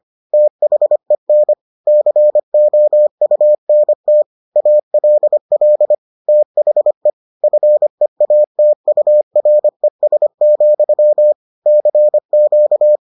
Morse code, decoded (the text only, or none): THEN COUNT ALL THE FEATURES, CQ